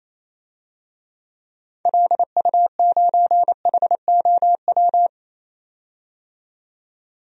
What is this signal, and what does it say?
Morse code, heard as a keyed tone, about 28 wpm, 700 Hz. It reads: LU9HOW
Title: DXCC-100 Call Signs - 6 Characters - 28wpm